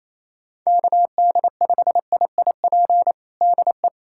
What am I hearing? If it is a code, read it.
Morse code, heard as a keyed tone, about 28 wpm, 700 Hz. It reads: KD5IIP DE
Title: USA Callsign DE Callsign - 6 Characters - 28wpm